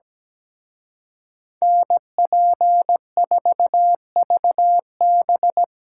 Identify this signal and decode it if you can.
Morse code, heard as a keyed tone, about 20 wpm, 700 Hz. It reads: NP4VB
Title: Speed Racing - USA 2x2 Call Signs 17wpm